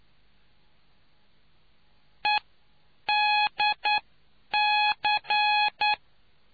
Morse code, heard as a keyed tone, about 9 wpm, spaced as for 8 wpm, 840 Hz. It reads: EDC